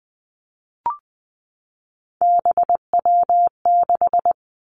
Morse code, heard as a keyed tone, about 20 wpm, 700 Hz. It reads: BW6
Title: Three Challenging Letter-Number Combinations 20wpm